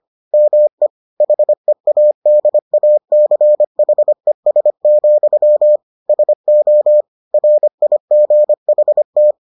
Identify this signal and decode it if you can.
Morse code, heard as a keyed tone, about 25 wpm, 600 Hz. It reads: ME HEADACHES, SO RIGHT